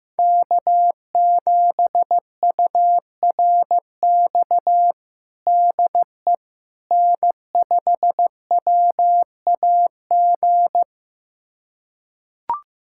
Morse code, heard as a keyed tone, about 15 wpm, 700 Hz. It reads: K7URX DE N5WAG
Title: USA Callsign DE Callsign - 5 Characters - 15wpm